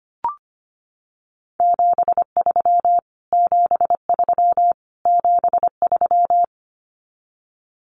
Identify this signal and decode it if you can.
Morse code, heard as a keyed tone, about 25 wpm, 700 Hz. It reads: 73 73 73
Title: Abbreviations Repeated 3x - 25wpm